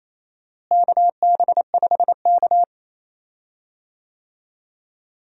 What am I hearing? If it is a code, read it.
Morse code, heard as a keyed tone, about 27 wpm, 700 Hz. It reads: KB5K
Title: USA Call Signs - 4 Characters - Repeated 3x - 28wpm